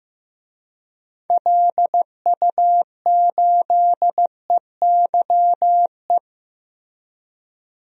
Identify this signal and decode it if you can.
Morse code, heard as a keyed tone, about 15 wpm, 700 Hz. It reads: LU8EYE
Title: DXCC-100 Call Signs - 6 Characters - Repeated 2x - 15wpm